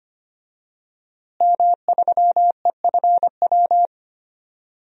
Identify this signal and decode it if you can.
Morse code, heard as a keyed tone, about 25 wpm, 700 Hz. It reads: M3EFW
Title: DXCC-100 Call Signs - 5 Characters - Repeated 2x - 25wpm